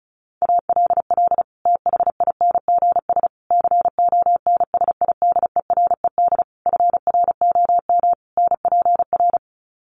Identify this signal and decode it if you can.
Morse code, heard as a keyed tone, about 34 wpm, 700 Hz. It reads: ALL THINGS CONSIDERED FROM NPR
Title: Sets of 5 Words Encore 35wpm